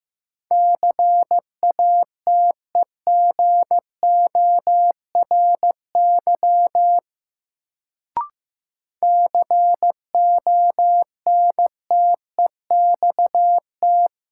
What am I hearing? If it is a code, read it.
Morse code, heard as a keyed tone, about 16 wpm, 700 Hz. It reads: CATEGORY CONTEXT
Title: Sentences from Top 1000 Words - Review Part 7 - New Word Drill - 15wpm